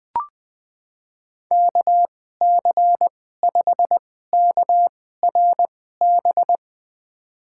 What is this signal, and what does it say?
Morse code, heard as a keyed tone, about 21 wpm, spaced as for 15 wpm, 700 Hz. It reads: KC5KRB
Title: USA Call Signs - 6 Characters - 2x Character Spacing - 20wpm